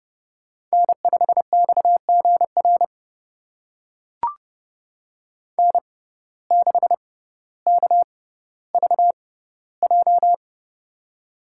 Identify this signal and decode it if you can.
Morse code, heard as a keyed tone, about 30 wpm, spaced as for 15 wpm, 700 Hz. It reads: N5XGR N6KVJ
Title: USA Call Signs - 5 Characters - 6x Character Spacing - 30wpm